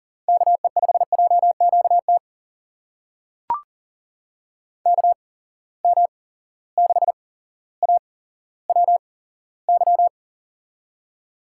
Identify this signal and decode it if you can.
Morse code, heard as a keyed tone, about 40 wpm, spaced as for 17 wpm, 700 Hz. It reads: KE5JQT KM6AWY
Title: USA Call Signs - 6 Characters - 8x Character Spacing - 40wpm